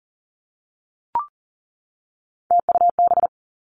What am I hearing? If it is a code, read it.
Morse code, heard as a keyed tone, about 41 wpm, 700 Hz. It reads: TUB
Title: Three Challenging Letter-Number Combinations 40wpm